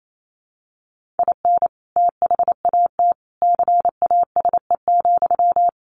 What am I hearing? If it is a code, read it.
Morse code, heard as a keyed tone, about 28 wpm, 700 Hz. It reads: IN THAT CASE,
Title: Sentences from Top 1400 Words 28wpm